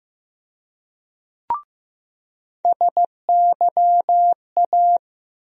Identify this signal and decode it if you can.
Morse code, heard as a keyed tone, about 15 wpm, 700 Hz. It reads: SYA
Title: USA Call Sign Postfixes - 3 Letters 15wpm